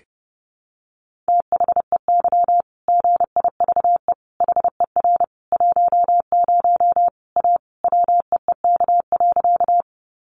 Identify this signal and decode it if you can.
Morse code, heard as a keyed tone, about 30 wpm, 700 Hz. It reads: THEY GIVE HER 10 A WEEK.
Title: Easy Phrases 30wpm